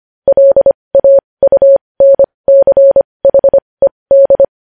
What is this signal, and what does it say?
Morse code, heard as a keyed tone, about 25 wpm, spaced as for 20 wpm, 550 Hz. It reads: LAUNCHED
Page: news-headlines-in-morse-code-at-20-wpm